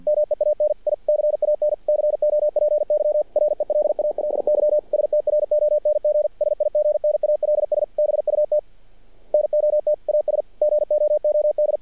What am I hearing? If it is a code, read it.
Morse code, heard as a keyed tone, about 50 wpm, 600 Hz. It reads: MEAN I CAN COPY RELALY STRONG SIGNALS BUT NOT AS GOOD
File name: archived / CRSnet